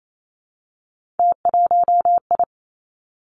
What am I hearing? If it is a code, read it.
Morse code, heard as a keyed tone, about 28 wpm, 700 Hz. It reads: T1I
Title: Three Challenging Letter-Number Combinations 28wpm